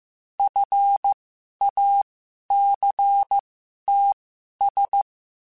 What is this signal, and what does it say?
Morse code, heard as a keyed tone, about 15 wpm, spaced as for 11 wpm, 800 Hz. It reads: FACTS
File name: Quotes / Quotes-mp3s / IgnoringFacts @13